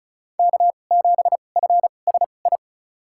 Morse code, heard as a keyed tone, about 36 wpm, spaced as for 25 wpm, 700 Hz. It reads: K7FSI
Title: USA Call Signs - 5 Characters - 2x Character Spacing - 35wpm